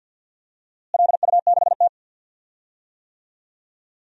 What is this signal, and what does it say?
Morse code, heard as a keyed tone, about 48 wpm, 700 Hz. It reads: LU6T
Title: DXCC-100 Call Signs - 4 Characters - Repeated 2x - 50wpm